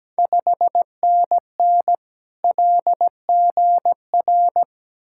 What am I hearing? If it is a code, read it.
Morse code, heard as a keyed tone, about 17 wpm, 700 Hz. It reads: 5NN LGR
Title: Ontario QSO Party Exchange 17wpm